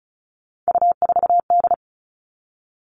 Morse code, heard as a keyed tone, about 34 wpm, 700 Hz. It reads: U4D
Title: All Cut Numbers - 3-Digit 35wpm